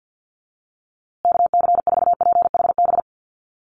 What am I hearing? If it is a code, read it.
Morse code, heard as a keyed tone, about 50 wpm, 700 Hz. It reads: KC4PHB
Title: USA Call Signs Repeated 2x 50wpm